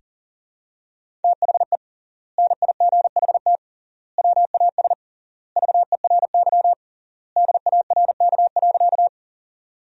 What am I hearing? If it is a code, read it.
Morse code, heard as a keyed tone, about 40 wpm, 700 Hz. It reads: THE NIGHT WAS VERY DARK.
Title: 3x Word-Spacing - Sentences from Top 500 Words - 40wpm